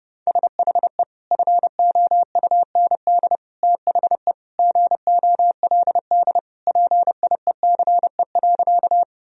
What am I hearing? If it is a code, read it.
Morse code, heard as a keyed tone, about 32 wpm, 700 Hz. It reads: SHE FOUND THE GOLD PIECE.
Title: Sentences from Top 1000 Words - Review Part 3 - 30wpm